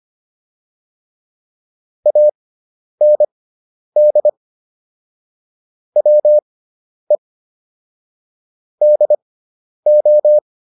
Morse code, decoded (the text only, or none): AND WE DO